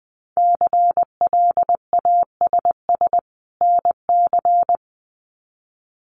Morse code, heard as a keyed tone, about 20 wpm, 700 Hz. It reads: CLASS NC
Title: Winter Field Day Exchanges 20wpm